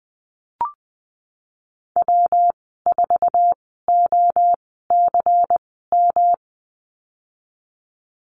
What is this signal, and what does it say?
Morse code, heard as a keyed tone, about 20 wpm, spaced as for 15 wpm, 700 Hz. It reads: W4OCM
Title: 186 Lesson 30g - Letter b - Call Signs 20wpm